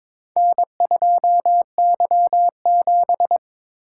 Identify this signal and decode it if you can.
Morse code, heard as a keyed tone, about 22 wpm, 700 Hz. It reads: N2Y7
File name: practice / 22wpm / Four Letter-Number Combinations - 22wpm